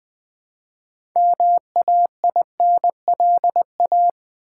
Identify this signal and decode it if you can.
Morse code, heard as a keyed tone, about 20 wpm, 700 Hz. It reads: MAINLA
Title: Build A Word - 3 Letters At A Time - 8-Letter Words - 20wpm